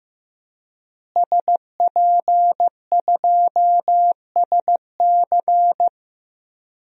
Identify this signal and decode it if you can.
Morse code, heard as a keyed tone, about 15 wpm, 700 Hz. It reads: SP2SC
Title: DXCC-100 Call Signs - 5 Characters - Repeated 3x - 15wpm